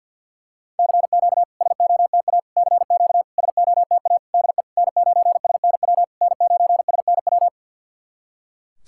Morse code, heard as a keyed tone, about 49 wpm, 690 Hz. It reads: CQ SOTA CQ SOTA DE N9SNW N9SNW
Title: SOTA QSO Practice 50wpm